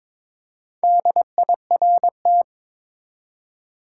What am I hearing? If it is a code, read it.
Morse code, heard as a keyed tone, about 22 wpm, 700 Hz. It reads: DIRT
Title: Four Letter Words - 22wpm